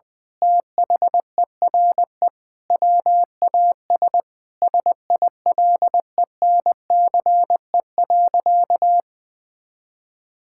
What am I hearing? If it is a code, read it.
Morse code, heard as a keyed tone, about 21 wpm, 700 Hz. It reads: THERE WAS SILENCE.
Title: Sentences with 5-Word Phrases - Spoken Beginning - 20wpm